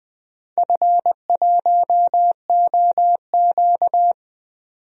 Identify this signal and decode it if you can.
Morse code, heard as a keyed tone, about 20 wpm, 700 Hz. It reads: F1OQ
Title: DXCC-100 Call Signs - 4 Characters - 20wpm